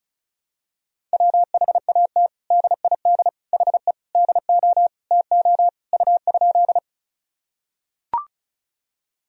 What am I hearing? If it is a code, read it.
Morse code, heard as a keyed tone, about 34 wpm, 700 Hz. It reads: WHAT DID HE DO TO U?
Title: Sentences from Top 100 Words and Abbreviations 35wpm